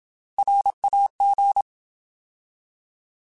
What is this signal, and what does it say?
Morse code, heard as a keyed tone, about 27 wpm, 800 Hz. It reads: RAG
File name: Alphabetical / Alphabetical mp3s / Rs @25